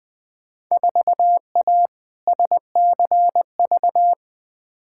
Morse code, heard as a keyed tone, about 20 wpm, 700 Hz. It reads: 4A SCV
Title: Field Day Exchanges - 20wpm